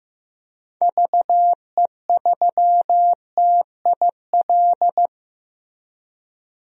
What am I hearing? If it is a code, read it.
Morse code, heard as a keyed tone, about 15 wpm, 700 Hz. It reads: VE3TIL
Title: DXCC-100 Call Signs - 6 Characters - Repeated 3x - 15wpm